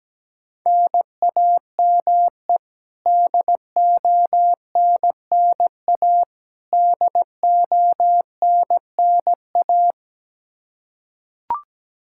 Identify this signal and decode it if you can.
Morse code, heard as a keyed tone, about 17 wpm, 700 Hz. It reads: NAME DONNA DONNA
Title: K1USN SST QSO Practice 17wpm